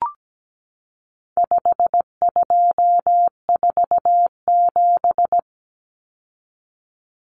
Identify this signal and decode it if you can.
Morse code, heard as a keyed tone, about 17 wpm, 700 Hz. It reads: 5247
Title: Common Cut Numbers - 4-Digit 17wpm